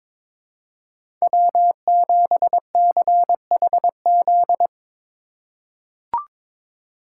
Morse code, heard as a keyed tone, about 22 wpm, 700 Hz. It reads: W7CHZ
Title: USA Call Signs - 5 Characters - 4x Character Spacing - 22wpm